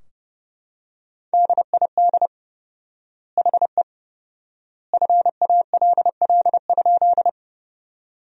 Morse code, heard as a keyed tone, about 30 wpm, 700 Hz. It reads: DID HE FALL?
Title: 4x Word-Spacing - Sentences from Top 500 Words - 30wpm